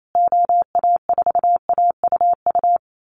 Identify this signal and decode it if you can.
Morse code, heard as a keyed tone, about 28 wpm, 700 Hz. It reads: OA4AUU
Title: USA Call Signs Repeated 2x 28wpm